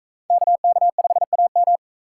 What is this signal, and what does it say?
Morse code, heard as a keyed tone, about 41 wpm, 700 Hz. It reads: KK5AM